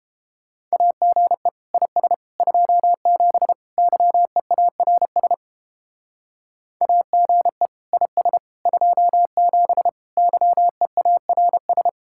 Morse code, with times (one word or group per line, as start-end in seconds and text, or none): 0.7-1.5 AGE
1.7-2.1 IS
2.4-3.5 27
3.8-5.3 YEARS
6.8-7.7 AGE
7.9-8.4 IS
8.7-9.9 27
10.2-11.9 YEARS